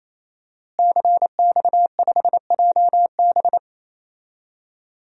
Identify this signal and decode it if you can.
Morse code, heard as a keyed tone, about 28 wpm, 700 Hz. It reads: CX5JB
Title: DXCC-100 Call Signs - 5 Characters - Repeated 3x - 28wpm